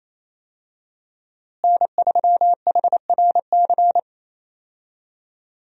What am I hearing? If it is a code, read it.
Morse code, heard as a keyed tone, about 28 wpm, 700 Hz. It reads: N3HRC
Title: USA Call Signs - 5 Characters - Repeated 3x - 28wpm